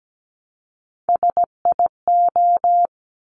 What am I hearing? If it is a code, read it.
Morse code, heard as a keyed tone, about 17 wpm, 690 Hz. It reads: SIO
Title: Top Three Letter Combinations in Words 17wpm